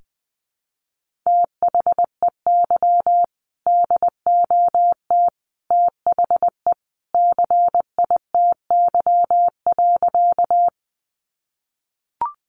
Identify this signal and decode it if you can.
Morse code, heard as a keyed tone, about 20 wpm, 700 Hz. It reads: THEY DOT THE CITY.
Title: Sentences from Top 1500 Words 20wpm